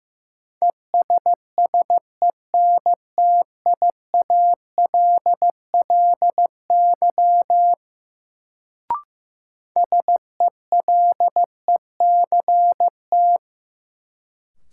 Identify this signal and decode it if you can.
Morse code, heard as a keyed tone, about 15 wpm, 700 Hz. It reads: ESSENTIALLY SELECT
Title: Sentences from Top 1300 Words - New Word Drill Alt - 15wpm